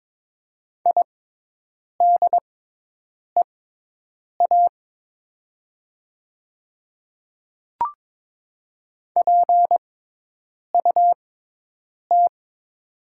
Morse code, heard as a keyed tone, about 22 wpm, spaced as for 8 wpm, 700 Hz. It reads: IDEA PUT
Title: 6x Character Spacing - Top 300 Words - 22wpm